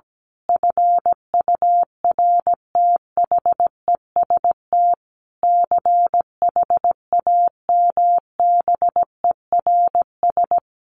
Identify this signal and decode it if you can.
Morse code, heard as a keyed tone, about 17 wpm, 700 Hz. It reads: FURTHEST CHAMBERS
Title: A Sound Of Thunder 17wpm